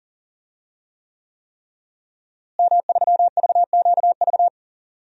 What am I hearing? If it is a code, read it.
Morse code, heard as a keyed tone, about 38 wpm, 700 Hz. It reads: M3VQV
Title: DXCC-100 Call Signs - 5 Characters - Repeated 3x - 40wpm